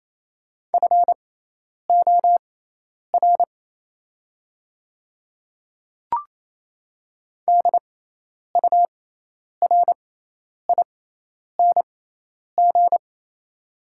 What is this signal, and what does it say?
Morse code, heard as a keyed tone, about 28 wpm, spaced as for 10 wpm, 700 Hz. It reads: FOR DURING
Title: 6x Character Spacing - Top 500 Words - 28wpm